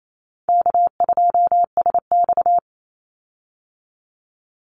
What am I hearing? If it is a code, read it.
Morse code, heard as a keyed tone, about 28 wpm, 700 Hz. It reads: K2SX
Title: Call Signs Repeated 2x - 28wpm